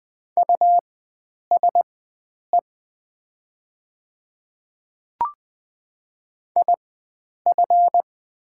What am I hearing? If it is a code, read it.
Morse code, heard as a keyed tone, about 20 wpm, spaced as for 9 wpm, 690 Hz. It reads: USE IF
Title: 4x Character Spacing - Top 100 Words - 20wpm